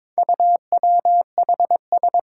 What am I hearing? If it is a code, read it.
Morse code, heard as a keyed tone, about 22 wpm, 700 Hz. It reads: UWHS
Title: Four Challenging Letter-Number Combinations 22wpm